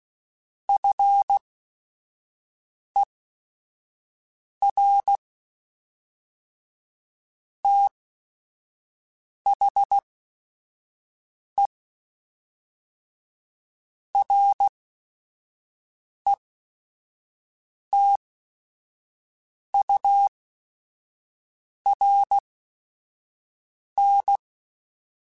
Morse code, heard as a keyed tone, about 16 wpm, spaced as for 5 wpm, 800 Hz. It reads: FER THE RETURN
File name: QSOs / QSOs-mp3s / kc7sac@5